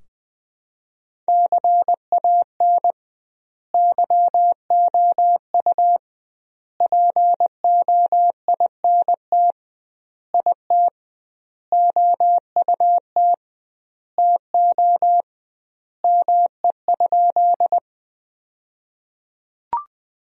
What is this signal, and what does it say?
Morse code, heard as a keyed tone, about 20 wpm, 700 Hz. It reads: CAN YOU POINT IT OUT TO ME?
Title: 2x Word-Spacing - Sentences from Top 200 Words - 20wpm